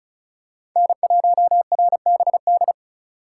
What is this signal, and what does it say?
Morse code, heard as a keyed tone, about 36 wpm, 700 Hz. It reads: N1RBD
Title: K1USN SST QSO Practice - 35wpm